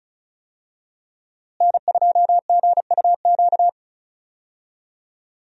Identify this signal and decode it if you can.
Morse code, heard as a keyed tone, about 36 wpm, 700 Hz. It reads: N2GUQ
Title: USA Call Signs - 5 Characters - Repeated 2x - 35wpm